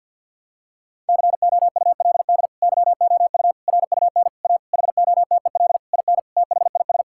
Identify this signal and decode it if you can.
Morse code, heard as a keyed tone, about 50 wpm, 700 Hz. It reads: COULD YOU RUN A HOTEL IN THIS
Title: Sentences from Top 1600 Words 50wpm